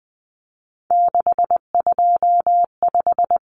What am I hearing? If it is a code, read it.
Morse code, heard as a keyed tone, about 20 wpm, 700 Hz. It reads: 625
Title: Common Cut Numbers - 3-Digit 20wpm